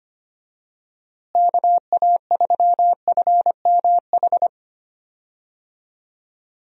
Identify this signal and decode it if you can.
Morse code, heard as a keyed tone, about 25 wpm, 700 Hz. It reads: KA3FMH
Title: USA Call Signs - 6 Characters - Repeated 2x - 25wpm